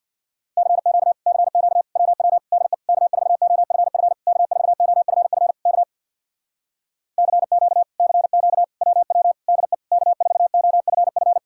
Morse code, heard as a keyed tone, about 64 wpm, 700 Hz. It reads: CQ CQ WW DE K4OFF K4OFF K CQ CQ WW DE K4OFF